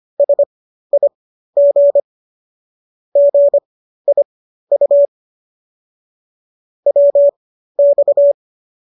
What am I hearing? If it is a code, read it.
Morse code, heard as a keyed tone, about 25 wpm, spaced as for 13 wpm, 570 Hz. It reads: SIG GIU WX